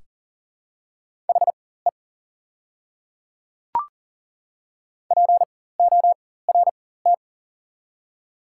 Morse code, heard as a keyed tone, about 40 wpm, spaced as for 19 wpm, 700 Hz. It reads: HE PORT
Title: 4x Character Spacing - Top 300 Words - 40wpm